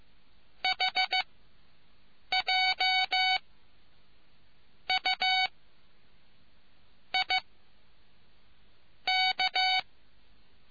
Morse code, heard as a keyed tone, about 15 wpm, spaced as for 5 wpm, 750 Hz. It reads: HJUIK